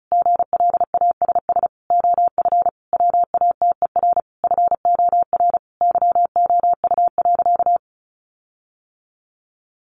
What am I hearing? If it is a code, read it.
Morse code, heard as a keyed tone, about 34 wpm, 700 Hz. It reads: GLASS OF WATER FOR YOU.
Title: Sentences from Top 1300 Words 35wpm